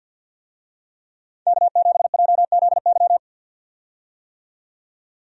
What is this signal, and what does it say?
Morse code, heard as a keyed tone, about 47 wpm, 700 Hz. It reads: K7JZY
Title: USA Call Signs - 5 Characters - 50wpm